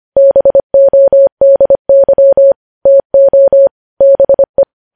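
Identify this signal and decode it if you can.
Morse code, heard as a keyed tone, about 25 wpm, 550 Hz. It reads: BODY TO BE
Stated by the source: Morse code news headlines at 25 words per minute.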